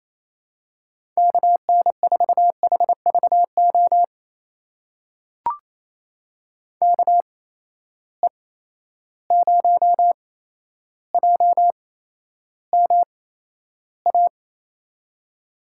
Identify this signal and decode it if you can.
Morse code, heard as a keyed tone, about 28 wpm, spaced as for 12 wpm, 700 Hz. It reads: KN4HVO KE0JMA
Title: USA Call Signs - 6 Characters - 8x Character Spacing - 28wpm